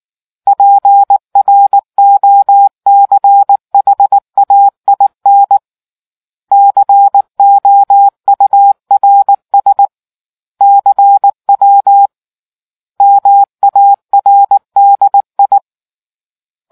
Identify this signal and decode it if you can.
Morse code, heard as a keyed tone, about 18 wpm, 800 Hz. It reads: PROCHAIN COURS CW MARDI